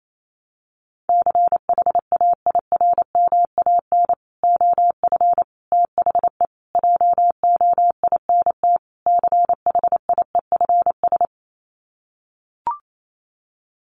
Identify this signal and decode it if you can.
Morse code, heard as a keyed tone, about 28 wpm, 700 Hz. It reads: CHAIRMAN OF THE JOINT CHIEFS
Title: Sets of 5 Words Encore 28wpm